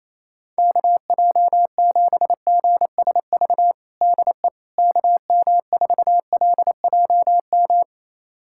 Morse code, heard as a keyed tone, about 28 wpm, 700 Hz. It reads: KJ7GSV DE KM4LJM
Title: USA Callsign DE Callsign - 6 Characters - 28wpm